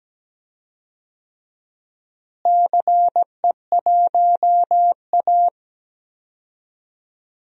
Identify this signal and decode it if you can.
Morse code, heard as a keyed tone, about 17 wpm, 700 Hz. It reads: CE1A
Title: DXCC-100 Call Signs - 4 Characters - Repeated 2x - 17wpm